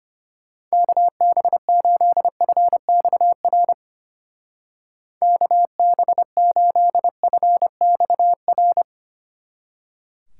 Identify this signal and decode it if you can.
Morse code, heard as a keyed tone, about 31 wpm, 700 Hz. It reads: KB8FXR KB8FXR